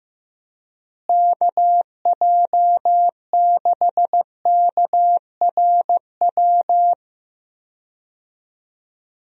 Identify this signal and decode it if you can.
Morse code, heard as a keyed tone, about 15 wpm, 700 Hz. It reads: KJ6KRW
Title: USA Call Signs - 6 Characters - Repeated 3x - 15wpm